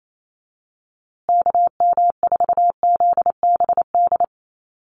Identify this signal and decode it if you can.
Morse code, heard as a keyed tone, about 28 wpm, 700 Hz. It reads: KM4ZBD
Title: USA Call Signs Repeated 3x 28wpm